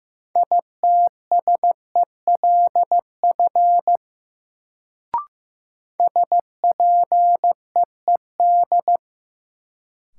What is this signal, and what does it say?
Morse code, heard as a keyed tone, about 15 wpm, 700 Hz. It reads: ITSELF SPEED
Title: Sentences from Top 800 Words - New Word Drill - 15wpm